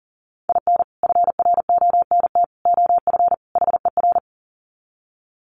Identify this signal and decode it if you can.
Morse code, heard as a keyed tone, about 40 wpm, 700 Hz. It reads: IN FRONT OF HER
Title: Sets of 4 Words 40wpm